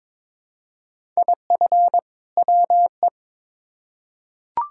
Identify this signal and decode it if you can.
Morse code, heard as a keyed tone, about 22 wpm, 700 Hz. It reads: IF WE
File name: Build A Phrase - 2-Word Phrases - 22wpm